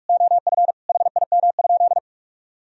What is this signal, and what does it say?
Morse code, heard as a keyed tone, about 46 wpm, 700 Hz. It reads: OF HIM?
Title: Sentences from Top 1000 Words - Review Part 3 - 45wpm